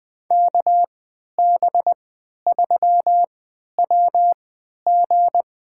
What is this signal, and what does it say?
Morse code, heard as a keyed tone, about 20 wpm, spaced as for 11 wpm, 700 Hz. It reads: KB3WG
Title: USA Call Signs - 3x Character Spacing - 20wpm